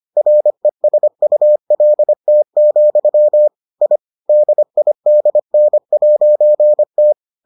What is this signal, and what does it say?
Morse code, heard as a keyed tone, about 25 wpm, 600 Hz. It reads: RESULT, I DIDN'T